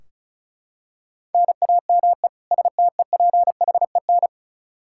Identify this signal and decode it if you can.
Morse code, heard as a keyed tone, about 34 wpm, 700 Hz. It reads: NAME STEPHEN
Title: Practice QSO - Exchange 1 - Beginning - 35wpm